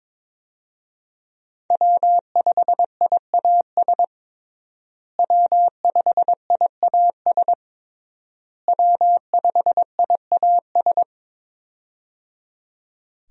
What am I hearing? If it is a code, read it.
Morse code, heard as a keyed tone, about 22 wpm, 700 Hz. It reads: W5IAS W5IAS W5IAS
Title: USA Call Signs - 5 Characters - Repeated 3x - 22wpm